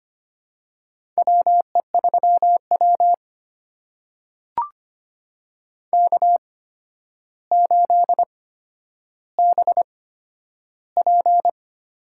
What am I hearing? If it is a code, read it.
Morse code, heard as a keyed tone, about 25 wpm, spaced as for 11 wpm, 700 Hz. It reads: WE3W K8BP
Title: USA Call Signs - 4 Characters - 8x Character Spacing - 25wpm